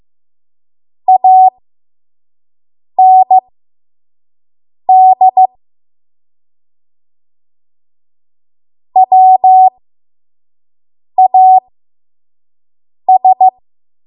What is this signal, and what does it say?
Morse code, heard as a keyed tone, about 15 wpm, spaced as for 5 wpm, 750 Hz. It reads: AND WAS
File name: Morse / 05WPM